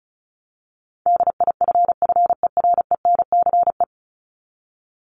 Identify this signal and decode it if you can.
Morse code, heard as a keyed tone, about 34 wpm, 700 Hz. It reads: DIFFERENCE
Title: ES:S Suffix - Top 100 Words with Base Words 35wpm